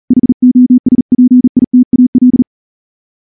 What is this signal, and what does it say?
Morse code, heard as a keyed tone, about 38 wpm, 250 Hz. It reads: HOSPITAL
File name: morse-training-40WPM